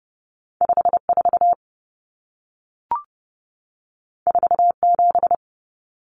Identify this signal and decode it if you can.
Morse code, heard as a keyed tone, about 30 wpm, 700 Hz. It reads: 54 47
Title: Numbers - 2-Digit 30wpm